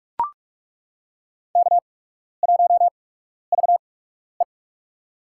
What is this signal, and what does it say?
Morse code, heard as a keyed tone, about 43 wpm, spaced as for 12 wpm, 700 Hz. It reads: K1VE
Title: USA Call Signs - 4 Characters - 8x Character Spacing - 45wpm